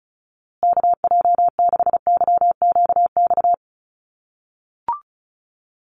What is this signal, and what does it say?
Morse code, heard as a keyed tone, about 34 wpm, 700 Hz. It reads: KJ6YQX
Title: USA Call Signs Repeated 3x 35wpm